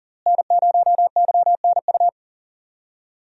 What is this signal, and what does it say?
Morse code, heard as a keyed tone, about 40 wpm, 700 Hz. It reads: N0YNU
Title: USA Call Signs - 5 Characters - Repeated 2x - 40wpm